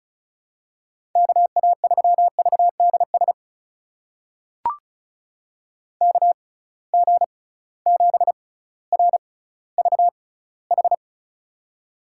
Morse code, heard as a keyed tone, about 36 wpm, spaced as for 18 wpm, 700 Hz. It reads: KA3VDS KG7RVH
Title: USA Call Signs - 6 Characters - 6x Character Spacing - 35wpm